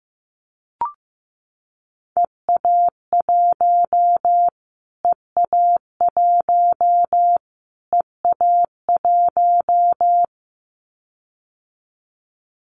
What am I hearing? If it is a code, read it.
Morse code, heard as a keyed tone, about 15 wpm, 700 Hz. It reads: EA1 EA1 EA1
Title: Callsign Prefixes for Top 25 DXCC - Repeated 3x - 15wpm